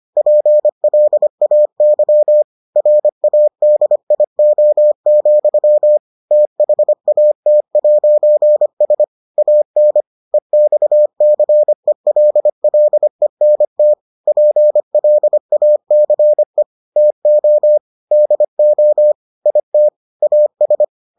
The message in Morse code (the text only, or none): PLAY RADIO, THAT'S AN EXCELLENT PLACE TO DO IT AS